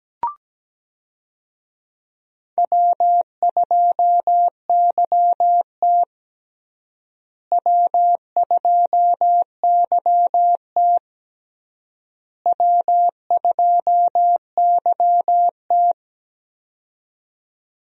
Morse code, heard as a keyed tone, about 17 wpm, 700 Hz. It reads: W2YT W2YT W2YT
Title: USA Call Signs - 4 Characters - Repeated 3x - 17wpm